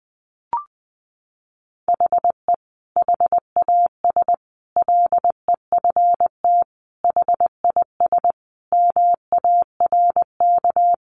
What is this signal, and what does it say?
Morse code, heard as a keyed tone, about 20 wpm, 700 Hz. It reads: HE HAS LEFT HIS MARK
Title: Sentences from Top 1100 Words 20wpm